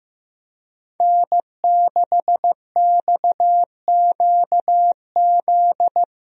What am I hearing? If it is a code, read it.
Morse code, heard as a keyed tone, about 15 wpm, 700 Hz. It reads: N6XQZ
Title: USA Call Signs - 5 Characters - Repeated 2x - 15wpm